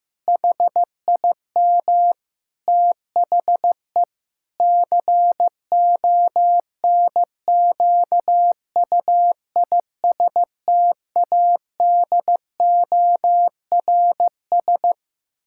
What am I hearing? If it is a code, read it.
Morse code, heard as a keyed tone, about 15 wpm, 700 Hz. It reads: HIM THE CONQUISTADORS
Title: Pangram Challenge v2 - 15wpm